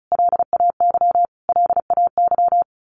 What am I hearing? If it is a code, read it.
Morse code, heard as a keyed tone, about 34 wpm, 700 Hz. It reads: LAY LAY
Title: Top 500 Words Repeated 35wpm